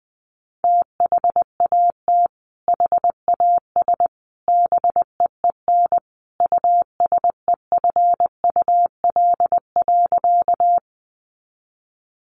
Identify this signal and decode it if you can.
Morse code, heard as a keyed tone, about 20 wpm, 700 Hz. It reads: THAT HAS BEEN USEFUL.
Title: Sentences from Top 700 Words 20wpm